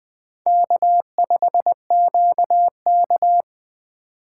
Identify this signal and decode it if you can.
Morse code, heard as a keyed tone, about 20 wpm, 700 Hz. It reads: K5QK
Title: USA Call Signs - 4 Characters - 3x Character Spacing - 20wpm